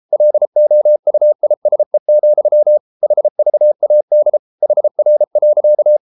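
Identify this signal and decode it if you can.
Morse code, heard as a keyed tone, about 33 wpm, 600 Hz. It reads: LOUISE, HVAD HR.
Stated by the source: Morse code at 33 wpm, QRQ.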